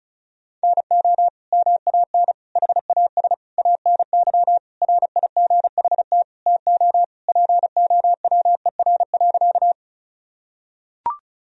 Morse code, heard as a keyed tone, about 36 wpm, 700 Hz. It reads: NO MAN HAS ANY RIGHT TO POWER.
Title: Sentences from Top 1000 Words - Review Part 2 - 35wpm